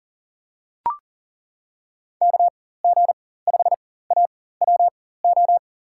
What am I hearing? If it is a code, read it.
Morse code, heard as a keyed tone, about 40 wpm, spaced as for 19 wpm, 700 Hz. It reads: KG5AWO
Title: USA Call Signs - 6 Characters - 4x Character Spacing - 40wpm